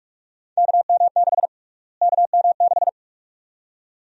Morse code, heard as a keyed tone, about 46 wpm, 700 Hz. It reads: KM6 KM6
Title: USA Call Sign Prefixes - Repeated 2x - 45wpm